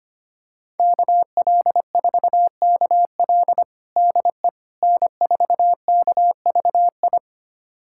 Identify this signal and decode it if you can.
Morse code, heard as a keyed tone, about 25 wpm, 700 Hz. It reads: KL4KL DE N4KVI
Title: USA Callsign DE Callsign - 5 Characters - 25wpm